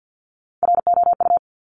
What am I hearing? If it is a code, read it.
Morse code, heard as a keyed tone, about 50 wpm, 700 Hz. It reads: ROU
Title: Top Three Letter Combinations in Words 50wpm